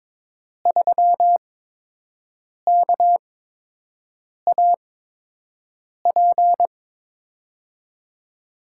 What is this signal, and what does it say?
Morse code, heard as a keyed tone, about 22 wpm, spaced as for 6 wpm, 700 Hz. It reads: 3KAP
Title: USA Call Signs - 5 Characters - 8x Character Spacing - 22wpm